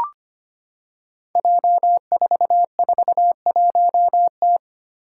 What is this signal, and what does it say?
Morse code, heard as a keyed tone, about 25 wpm, 700 Hz. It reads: J441T
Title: Five Challenging Letter-Number Combinations 25wpm